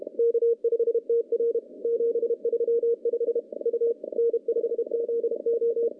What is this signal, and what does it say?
Morse code, heard as a keyed tone, about 33 wpm, 470 Hz. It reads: K5TR 735 U N5ZO
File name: contests / CW-SS-k5tr- -0600